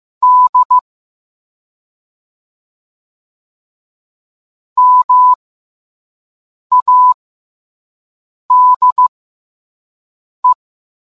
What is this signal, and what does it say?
Morse code, heard as a keyed tone, about 15 wpm, spaced as for 5 wpm, 1000 Hz. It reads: D MADE